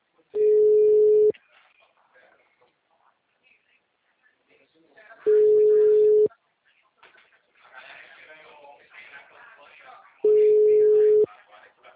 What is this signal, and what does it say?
Ringback tone (Europe, 425 Hz), with speech.
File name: out- -2006- - - .4841